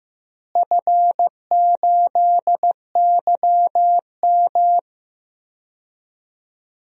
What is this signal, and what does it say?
Morse code, heard as a keyed tone, about 15 wpm, 700 Hz. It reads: F8YM
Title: DXCC-100 Call Signs - 4 Characters - Repeated 3x - 15wpm